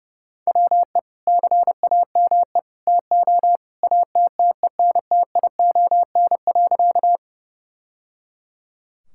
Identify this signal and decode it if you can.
Morse code, heard as a keyed tone, about 28 wpm, 700 Hz. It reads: WE CAME TO ATTENTION.
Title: Sentences from Top 1000 Words - Review Part 6 - 30wpm